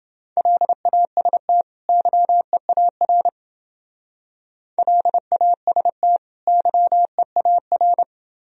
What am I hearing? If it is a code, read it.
Morse code, heard as a keyed tone, about 28 wpm, 700 Hz. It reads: LAST YEAR LAST YEAR